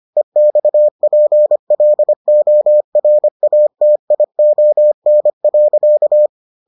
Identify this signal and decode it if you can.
Morse code, heard as a keyed tone, about 25 wpm, 600 Hz. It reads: EXPLORATION.